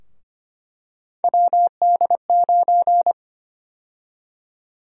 Morse code, heard as a keyed tone, about 25 wpm, 700 Hz. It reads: WD9
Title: USA Call Sign Prefixes - 25wpm